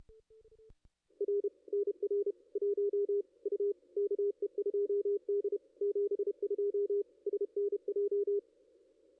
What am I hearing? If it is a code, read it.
Morse code, heard as a keyed tone, about 30 wpm, 410 Hz, with radio noise. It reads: R NR 1 U KE2D 72 SNJ
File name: contests / ss-cw-k5tr-op-n5rz -2300